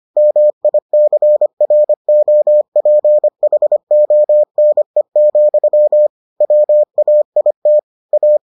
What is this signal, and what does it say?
Morse code, heard as a keyed tone, about 25 wpm, 600 Hz. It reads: MICROPHONE, WAIT A